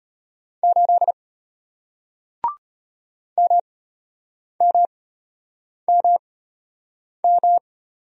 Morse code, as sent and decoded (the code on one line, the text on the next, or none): ---.. / -- -- -- --
8 MMMM